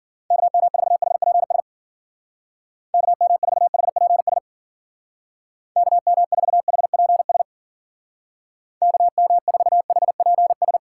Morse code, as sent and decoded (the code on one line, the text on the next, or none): -.- -- ....- .... .--. ... / -.- -- ....- .... .--. ... / -.- -- ....- .... .--. ... / -.- -- ....- .... .--. ...
KM4HPS KM4HPS KM4HPS KM4HPS